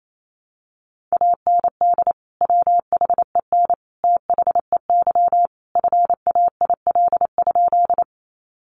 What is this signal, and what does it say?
Morse code, heard as a keyed tone, about 28 wpm, 700 Hz. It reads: AND WHEN THEY FAIL?
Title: Sentences from Top 1100 Words 28wpm